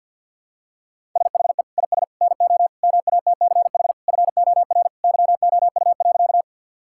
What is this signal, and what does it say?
Morse code, heard as a keyed tone, about 50 wpm, 700 Hz. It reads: SHE IS NO MATCH FOR YOU.
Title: Sentences from Top 1600 Words 50wpm